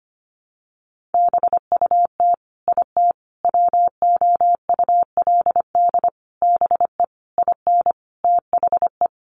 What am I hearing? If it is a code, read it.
Morse code, heard as a keyed tone, about 25 wpm, 700 Hz. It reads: BUT IT WOULD BE IN THE
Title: Sentences from Top 300 Words 25wpm